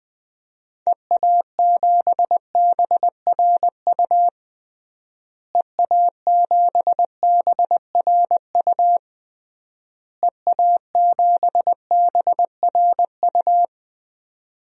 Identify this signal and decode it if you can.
Morse code, heard as a keyed tone, about 20 wpm, 700 Hz. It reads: EA7BRU EA7BRU EA7BRU
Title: DXCC-100 Call Signs - 6 Characters - Repeated 3x - 20wpm